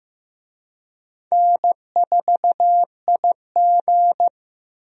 Morse code, heard as a keyed tone, about 15 wpm, 700 Hz. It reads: N4IG
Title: USA Call Signs - 4 Characters - Repeated 2x - 15wpm